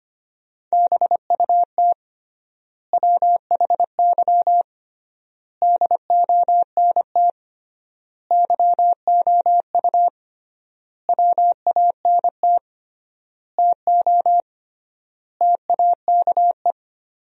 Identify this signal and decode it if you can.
Morse code, heard as a keyed tone, about 25 wpm, 700 Hz. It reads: BUT WHY DONT YOU WANT TO TAKE
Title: 3x Word-Spacing - Sentences from Top 300 Words - 25wpm